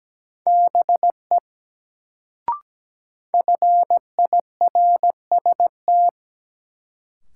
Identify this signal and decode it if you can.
Morse code, heard as a keyed tone, about 17 wpm, 700 Hz. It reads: BE FIRST
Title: Sentences from Top 100 Words - New Word Drill - 17wpm